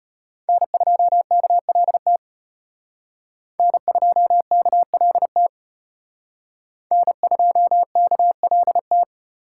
Morse code, heard as a keyed tone, about 34 wpm, 700 Hz. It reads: N2KLT N2KLT N2KLT